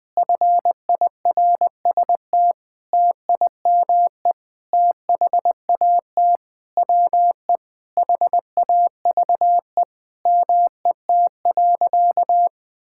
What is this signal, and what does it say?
Morse code, heard as a keyed tone, about 20 wpm, 700 Hz. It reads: FIRST TIME THAT WE HAVE MET.
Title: Sentences with 7-Word Phrases - 20wpm